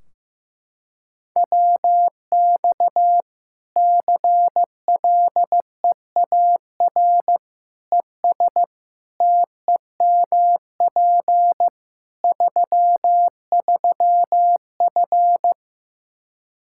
Morse code, heard as a keyed tone, about 15 wpm, 700 Hz. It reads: WX CLEAR ES TEMP 33F
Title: Practice QSO - Exchange 2 - Info - 15wpm